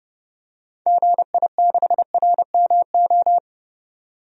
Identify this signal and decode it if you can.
Morse code, heard as a keyed tone, about 32 wpm, 700 Hz. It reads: GI6RMO
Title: DXCC-100 Call Signs - 6 Characters - 30wpm